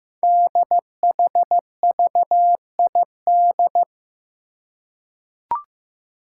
Morse code, heard as a keyed tone, about 15 wpm, 700 Hz. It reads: DHVID
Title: Code Groups - 5 Letters - 15wpm